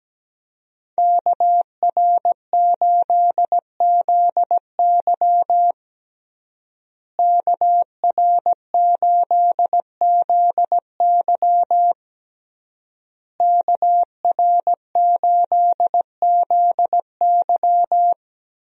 Morse code, heard as a keyed tone, about 17 wpm, 700 Hz. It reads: KR8ZY KR8ZY KR8ZY
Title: USA Call Signs - 5 Characters - Repeated 3x - 17wpm